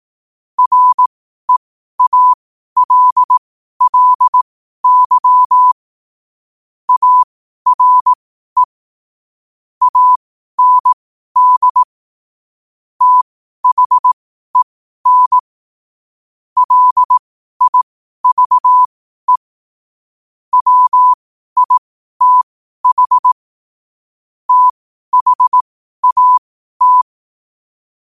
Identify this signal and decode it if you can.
Morse code, heard as a keyed tone, about 18 wpm, spaced as for 12 wpm, 1000 Hz. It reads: REALLY ARE AND THEN LIVE WITH THAT
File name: qotdfiles / 12wpm